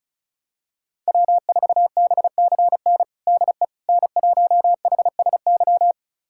Morse code, heard as a keyed tone, about 36 wpm, 700 Hz. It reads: W4BCN DE N1HSY
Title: USA Callsign DE Callsign - 5 Characters - 35wpm